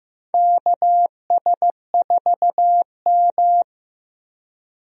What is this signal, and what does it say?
Morse code, heard as a keyed tone, about 15 wpm, 700 Hz. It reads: KS4M
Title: USA Call Signs - 4 Characters - Repeated 3x - 15wpm